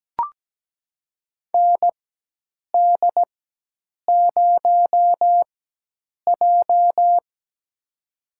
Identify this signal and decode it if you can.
Morse code, heard as a keyed tone, about 17 wpm, spaced as for 8 wpm, 700 Hz. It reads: ND0J
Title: USA Call Signs - 4x Character Spacing - 17wpm